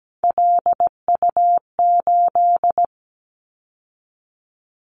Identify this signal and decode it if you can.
Morse code, heard as a keyed tone, about 17 wpm, 700 Hz. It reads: LU8
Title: Callsign Prefixes for Top 100 DXCC - 17wpm